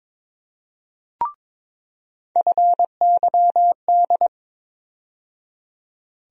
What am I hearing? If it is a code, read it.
Morse code, heard as a keyed tone, about 22 wpm, 700 Hz. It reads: FYD
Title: USA Call Sign Postfixes - 3 Letters 22wpm